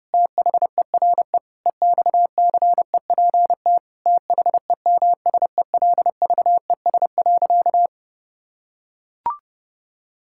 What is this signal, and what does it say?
Morse code, heard as a keyed tone, about 30 wpm, 700 Hz. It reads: THERE EXCEPT THEMSELVES.
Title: Sentences from Top 1600 Words 30wpm